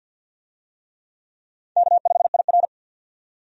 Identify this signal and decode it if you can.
Morse code, heard as a keyed tone, about 47 wpm, 700 Hz. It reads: K5IR
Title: USA Call Signs - 4 Characters - 50wpm